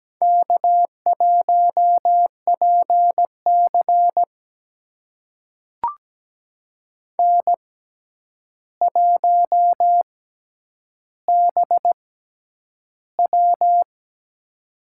Morse code, heard as a keyed tone, about 17 wpm, spaced as for 9 wpm, 700 Hz. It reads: K1PC N1BW
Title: USA Call Signs - 4 Characters - 6x Character Spacing - 17wpm